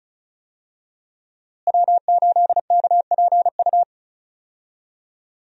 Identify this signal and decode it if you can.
Morse code, heard as a keyed tone, about 34 wpm, 700 Hz. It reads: W8KPU
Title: USA Call Signs - 5 Characters - Repeated 2x - 35wpm